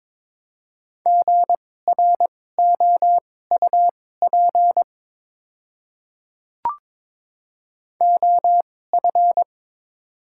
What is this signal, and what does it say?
Morse code, heard as a keyed tone, about 22 wpm, spaced as for 16 wpm, 700 Hz. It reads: GROUP OF
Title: 2x Character Spacing - Top 300 Words - 22wpm